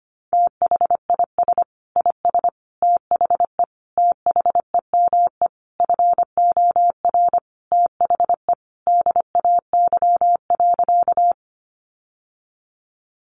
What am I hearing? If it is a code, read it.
Morse code, heard as a keyed tone, about 25 wpm, 700 Hz. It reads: THIS IS THE THEME FOR THE DAY.
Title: Sentences from Top 1100 Words 25wpm